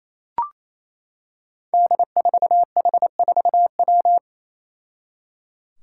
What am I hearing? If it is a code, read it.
Morse code, heard as a keyed tone, about 28 wpm, 700 Hz. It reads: D4H4W
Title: Five Challenging Letter-Number Combinations 28wpm